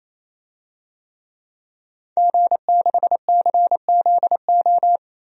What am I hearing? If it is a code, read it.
Morse code, heard as a keyed tone, about 28 wpm, 700 Hz. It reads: G6CZO
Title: DXCC-100 Call Signs - 5 Characters - 28wpm